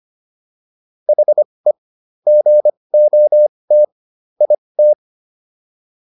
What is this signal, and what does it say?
Morse code, heard as a keyed tone, about 25 wpm, spaced as for 20 wpm, 600 Hz. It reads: HE GOT IT